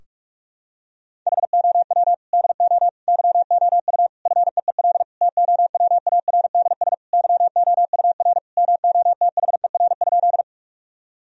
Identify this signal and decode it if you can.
Morse code, heard as a keyed tone, about 44 wpm, 700 Hz. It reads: HOW DO YOU FEEL TOWARDS YOUR MOTHER?
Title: Sentences from Top 1000 Words - Review Part 2 - 45wpm